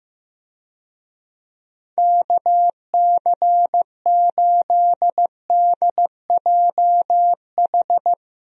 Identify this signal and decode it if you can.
Morse code, heard as a keyed tone, about 15 wpm, 700 Hz. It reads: KC8DJH
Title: USA Call Signs - 6 Characters - Repeated 3x - 15wpm